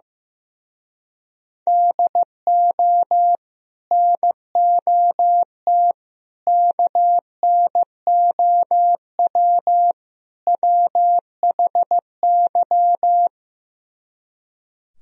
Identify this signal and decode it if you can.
Morse code, heard as a keyed tone, about 15 wpm, 700 Hz. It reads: DO NOT KNOW WHY
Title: Speed-Racing - Sets of 4 Words 15wpm